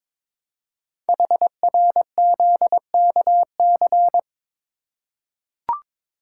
Morse code, heard as a keyed tone, about 22 wpm, 700 Hz. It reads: HRZKC
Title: Code Groups - 5 Letters - 22wpm